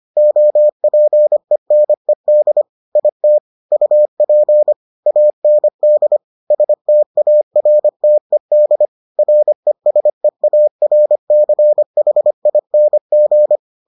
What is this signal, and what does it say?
Morse code, heard as a keyed tone, about 25 wpm, 600 Hz. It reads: OPENED IT UP AND STARTED RESEARCHING